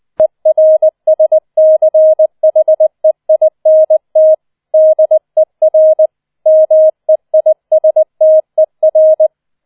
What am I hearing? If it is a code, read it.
Morse code, heard as a keyed tone, about 20 wpm, 630 Hz. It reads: ERSCHEINT DER MEISTER